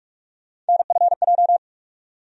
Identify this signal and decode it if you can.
Morse code, heard as a keyed tone, about 46 wpm, 700 Hz. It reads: NFJ
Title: USA Call Sign Postfixes - 3 Letters 45wpm